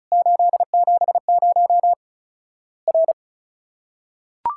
Morse code, heard as a keyed tone, about 36 wpm, 700 Hz. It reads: 870 R
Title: Worked All Europe DX Contest - QTC Exchange - 35wpm